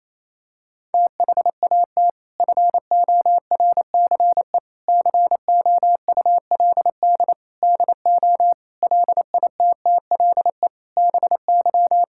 Morse code, heard as a keyed tone, about 28 wpm, 700 Hz. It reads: THAT FORCE COULD DO LITTLE BY
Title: Sentences from Top 1000 Words - Review Part 5 - 28wpm